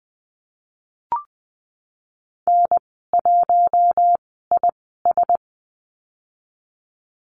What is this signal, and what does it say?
Morse code, heard as a keyed tone, about 20 wpm, spaced as for 15 wpm, 700 Hz. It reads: N1IS
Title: 048 Lesson 9g - Number 4 - Call Signs 20wpm